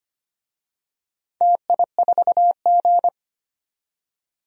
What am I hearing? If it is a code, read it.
Morse code, heard as a keyed tone, about 25 wpm, 700 Hz. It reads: TI4G
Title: DXCC-100 Call Signs - 4 Characters - 25wpm